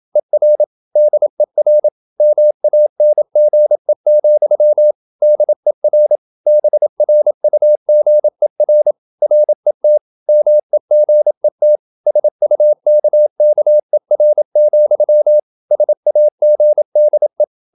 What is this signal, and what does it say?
Morse code, heard as a keyed tone, about 27 wpm, 600 Hz. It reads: ER DER MANGE, DER BRUGER RET MEGET SUKKER, SAGDE
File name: Ferie QRQ 27wpm